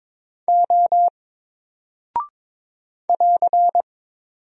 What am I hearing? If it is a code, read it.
Morse code, heard as a keyed tone, about 22 wpm, 700 Hz. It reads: O <AR>
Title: Single Character (Letter:Number:Punctuation:Prosign) 22wpm